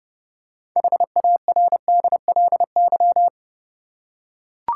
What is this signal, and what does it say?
Morse code, heard as a keyed tone, about 31 wpm, 700 Hz. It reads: HARDLY
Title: Six Letter Words - 30wpm